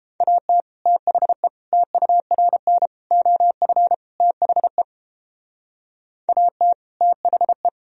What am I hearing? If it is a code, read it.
Morse code, heard as a keyed tone, about 31 wpm, 700 Hz. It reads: AT THE TURN OF THE AT THE